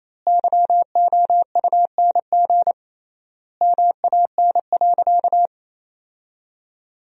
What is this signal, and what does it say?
Morse code, heard as a keyed tone, about 28 wpm, 700 Hz. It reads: YOUNG MAN.
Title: 3x Word-Spacing - Sentences from Top 400 Words - 28wpm